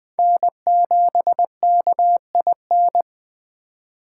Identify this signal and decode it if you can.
Morse code, heard as a keyed tone, about 19 wpm, 700 Hz. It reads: N7KIN